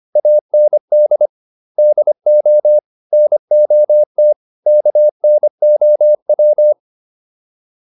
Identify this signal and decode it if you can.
Morse code, heard as a keyed tone, about 25 wpm, 600 Hz. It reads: AND DO NOT KNOW